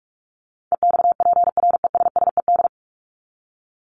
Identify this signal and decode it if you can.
Morse code, heard as a keyed tone, about 44 wpm, 700 Hz. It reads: EXPRESSED
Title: ED Suffix - Top 100 Words with Base Words 45wpm